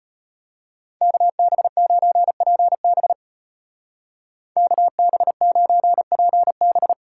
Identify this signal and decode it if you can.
Morse code, heard as a keyed tone, about 33 wpm, 700 Hz. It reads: KB9PB KB9PB